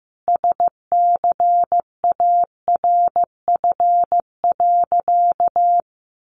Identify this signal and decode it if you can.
Morse code, heard as a keyed tone, about 15 wpm, 700 Hz. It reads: SCARF.
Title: Easy Phrases 15wpm